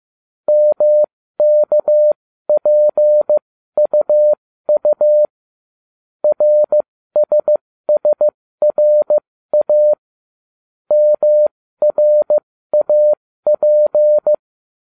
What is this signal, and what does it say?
Morse code, heard as a keyed tone, about 15 wpm, spaced as for 13 wpm, 600 Hz. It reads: MKPUU RSSRA MRAP